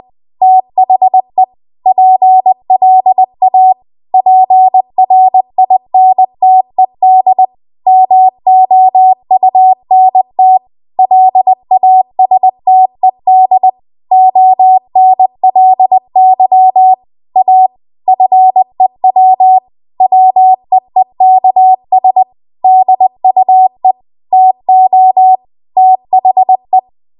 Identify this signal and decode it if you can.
Morse code, heard as a keyed tone, about 20 wpm, 750 Hz. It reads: THE PLA PRINTED MOUNT LASTED ONLY A FEW WEEKS DUE TO THE
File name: Archive / 20 WPM / 20WPM